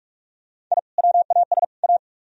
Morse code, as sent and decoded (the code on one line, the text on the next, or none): .. / .-- .- ... / .-
I WAS A